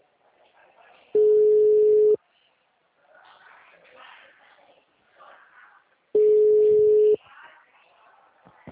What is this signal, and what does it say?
Ringback tone (Europe, 425 Hz), with speech.
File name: out- -2007- - - .75